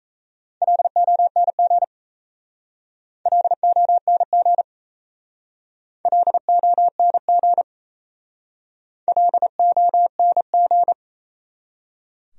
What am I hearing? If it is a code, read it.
Morse code, heard as a keyed tone, about 34 wpm, 700 Hz. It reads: LONG LONG LONG LONG
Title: Speed-Racing Top 300 Words 28wpm